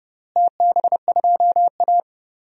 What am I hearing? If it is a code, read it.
Morse code, heard as a keyed tone, about 30 wpm, 700 Hz. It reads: TB2A
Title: DXCC-100 Call Signs - 4 Characters - Repeated 3x - 30wpm